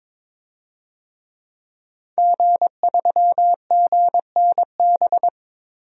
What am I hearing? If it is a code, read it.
Morse code, heard as a keyed tone, about 22 wpm, 700 Hz. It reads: G3GNB
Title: DXCC-100 Call Signs - 5 Characters - Repeated 3x - 22wpm